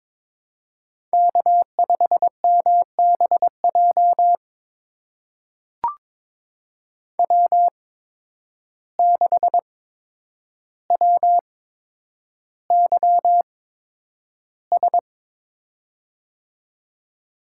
Morse code, heard as a keyed tone, about 22 wpm, spaced as for 9 wpm, 700 Hz. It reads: K5MBJ W6WYS
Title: USA Call Signs - 5 Characters - 8x Character Spacing - 22wpm